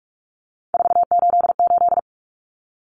Morse code, heard as a keyed tone, about 44 wpm, 700 Hz. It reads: 488
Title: Numbers - 3-Digit 45wpm